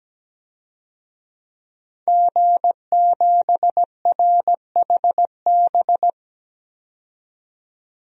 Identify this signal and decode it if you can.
Morse code, heard as a keyed tone, about 17 wpm, 700 Hz. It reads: G7RHB
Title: DXCC-100 Call Signs - 5 Characters - 17wpm